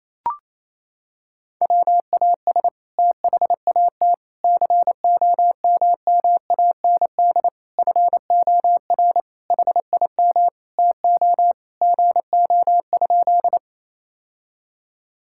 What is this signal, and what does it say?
Morse code, heard as a keyed tone, about 27 wpm, 700 Hz. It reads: WAS THAT COMMAND FOR HIM TO GO?
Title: Sentences from Top 1600 Words 28wpm